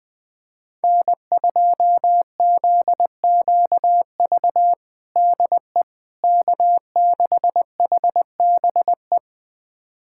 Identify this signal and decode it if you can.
Morse code, heard as a keyed tone, about 20 wpm, 700 Hz. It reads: N2ZQV DE K6HBE
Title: USA Callsign DE Callsign 20wpm